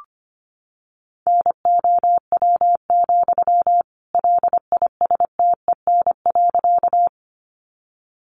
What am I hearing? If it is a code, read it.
Morse code, heard as a keyed tone, about 25 wpm, 700 Hz. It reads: NOW, LISTEN.
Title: Sentences from Top 400 Words 25wpm